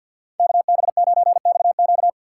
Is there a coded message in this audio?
Yes: Morse code from 0.4 to 2.1 s, heard as a keyed tone, about 63 wpm, 700 Hz. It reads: KB9XQ